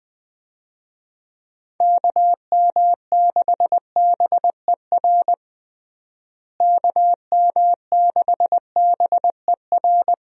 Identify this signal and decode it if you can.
Morse code, heard as a keyed tone, about 20 wpm, 700 Hz. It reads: KM6BER KM6BER
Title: USA Call Signs - 6 Characters - Repeated 3x - 20wpm